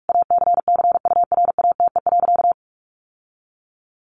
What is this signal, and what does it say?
Morse code, heard as a keyed tone, about 44 wpm, 700 Hz. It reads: ACCURATE.
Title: Sentences from Top 1200 Words 45wpm